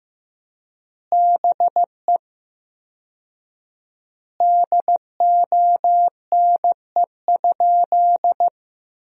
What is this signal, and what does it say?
Morse code, heard as a keyed tone, about 15 wpm, 700 Hz. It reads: BE DONE?
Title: 4x Word-Spacing - Sentences from Top 500 Words - 15wpm